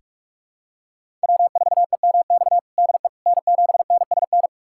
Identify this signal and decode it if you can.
Morse code, heard as a keyed tone, about 42 wpm, 700 Hz. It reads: W4EMX DE N7NSN
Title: USA Callsign DE Callsign - 5 Characters - 45wpm